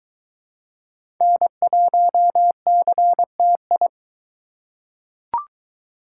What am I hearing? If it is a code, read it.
Morse code, heard as a keyed tone, about 22 wpm, 700 Hz. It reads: N1CTI